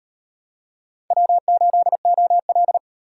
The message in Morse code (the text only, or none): W8OL